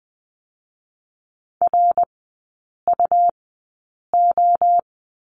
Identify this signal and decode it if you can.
Morse code, heard as a keyed tone, about 20 wpm, spaced as for 8 wpm, 700 Hz. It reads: RUO
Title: 100 Lesson 17e - Letter u - Three Characters 20wpm